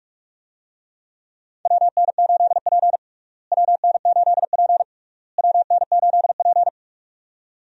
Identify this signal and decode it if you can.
Morse code, heard as a keyed tone, about 46 wpm, 700 Hz. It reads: WN8P WN8P WN8P
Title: USA Call Signs - 4 Characters - Repeated 3x - 45wpm